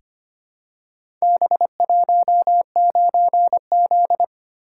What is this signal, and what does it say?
Morse code, heard as a keyed tone, about 25 wpm, 700 Hz. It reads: B19Z
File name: practice / 25wpm / Four Letter-Number Combinations - 25wpm